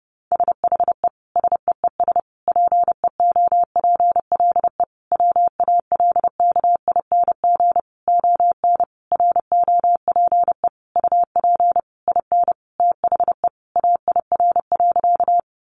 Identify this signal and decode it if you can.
Morse code, heard as a keyed tone, about 30 wpm, 700 Hz. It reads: SHE SEES PEOPLE WALKING ON ROPE UP IN THE AIR.
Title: Easy Phrases 30wpm